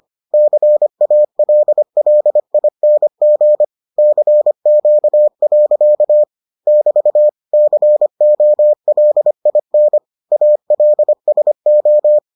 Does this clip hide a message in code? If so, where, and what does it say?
Yes: Morse code from 0.3 to 12.2 s, heard as a keyed tone, about 25 wpm, 600 Hz. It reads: CALLING CQ. <BT> COLIN ALSO